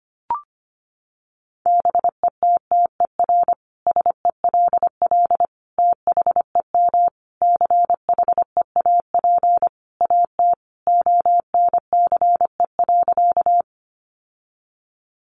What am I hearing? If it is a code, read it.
Morse code, heard as a keyed tone, about 25 wpm, 700 Hz. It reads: BETTER SELL THEM CHEAP AT ONCE.
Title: Sentences from Top 1200 Words 25wpm